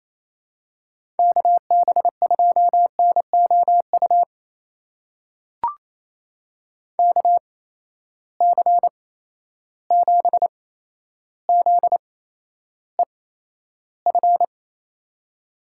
Morse code, heard as a keyed tone, about 28 wpm, spaced as for 12 wpm, 700 Hz. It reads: KB2NOU KC7ZEF
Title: USA Call Signs - 8x Character Spacing - 28wpm